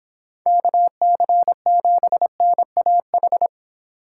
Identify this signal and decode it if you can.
Morse code, heard as a keyed tone, about 25 wpm, 700 Hz. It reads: KC7NAH